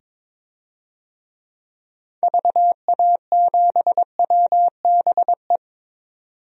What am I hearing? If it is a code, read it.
Morse code, heard as a keyed tone, about 22 wpm, 700 Hz. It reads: VA7WBE
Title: DXCC-100 Call Signs - 6 Characters - 22wpm